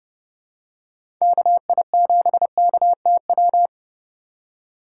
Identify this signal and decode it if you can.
Morse code, heard as a keyed tone, about 30 wpm, 700 Hz. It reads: KI7KTW